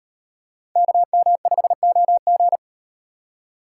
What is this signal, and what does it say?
Morse code, heard as a keyed tone, about 38 wpm, 700 Hz. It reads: KM5OG